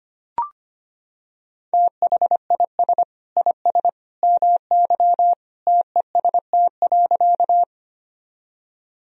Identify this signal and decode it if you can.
Morse code, heard as a keyed tone, about 25 wpm, 700 Hz. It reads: THIS IS MY TEST.
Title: Sentences from Top 1000 Words - Review Part 3 - 25wpm